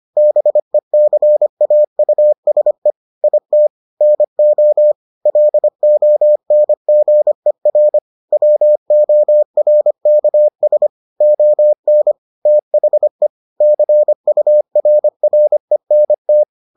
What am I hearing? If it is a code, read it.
Morse code, heard as a keyed tone, about 25 wpm, 600 Hz. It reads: BECAUSE IT NO LONGER WORKS ON THE CURRENT